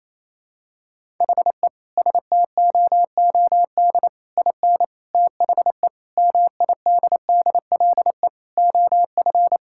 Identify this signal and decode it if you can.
Morse code, heard as a keyed tone, about 28 wpm, 700 Hz. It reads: HE STOOD IN THE MIDDLE OF
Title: Sentences from Top 1000 Words - Review Part 6 - 28wpm